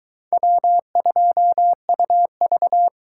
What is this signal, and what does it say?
Morse code, heard as a keyed tone, about 22 wpm, 700 Hz. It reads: W2UV